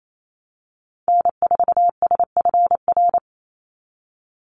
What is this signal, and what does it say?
Morse code, heard as a keyed tone, about 28 wpm, 700 Hz. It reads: N4SFR
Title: NA SOTA Activator Callsigns 28wpm